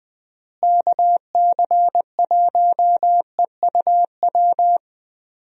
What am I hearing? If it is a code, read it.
Morse code, heard as a keyed tone, about 20 wpm, 700 Hz. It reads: KC1EUW
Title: USA Call Signs - 6 Characters - Repeated 3x - 20wpm